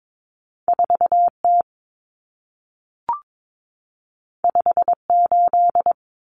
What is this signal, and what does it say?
Morse code, heard as a keyed tone, about 22 wpm, 700 Hz. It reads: 4T 58
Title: Common Cut Numbers - 2-Digit 22wpm